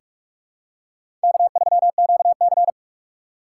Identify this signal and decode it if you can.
Morse code, heard as a keyed tone, about 45 wpm, 700 Hz. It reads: K3QC
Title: USA Call Signs - 4 Characters - 3x Character Spacing - 45wpm